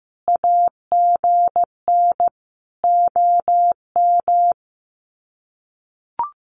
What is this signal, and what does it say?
Morse code, heard as a keyed tone, about 15 wpm, 700 Hz. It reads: AGN OM
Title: Common QSO Phrases 15wpm